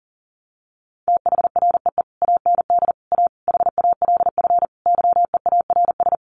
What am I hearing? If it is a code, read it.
Morse code, heard as a keyed tone, about 40 wpm, 700 Hz. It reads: THREE AND A HALF YEARS
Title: Sets of 5 Words Encore2 40wpm